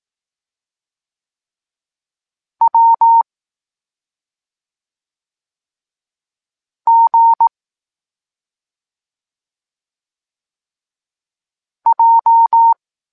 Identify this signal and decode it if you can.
Morse code, heard as a keyed tone, about 18 wpm, 920 Hz. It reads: W G J